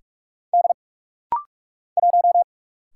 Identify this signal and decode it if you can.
Morse code, heard as a keyed tone, about 45 wpm, 700 Hz. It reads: D 1
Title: Single Letter-Number - Rapid-Fire 45wpm